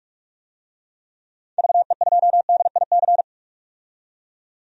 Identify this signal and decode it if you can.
Morse code, heard as a keyed tone, about 44 wpm, 690 Hz. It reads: VE2DIC
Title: DXCC-100 Call Signs - 6 Characters - 45wpm